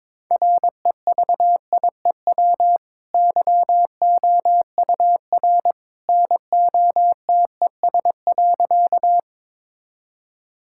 Morse code, heard as a keyed tone, about 21 wpm, 700 Hz. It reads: REVIEW YOUR NOTES.
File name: practice / 22wpm / Easy Sentences with Hints - 22wpm